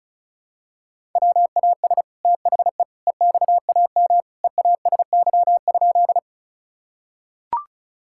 Morse code, heard as a keyed tone, about 36 wpm, 700 Hz. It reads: WAS THE EXAM EASY?
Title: Sentences from Top 1000 Words - Review Part 5 - 35wpm